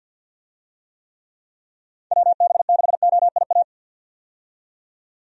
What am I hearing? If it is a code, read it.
Morse code, heard as a keyed tone, about 51 wpm, 700 Hz. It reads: WB6OIA
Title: USA Call Signs - 6 Characters - 50wpm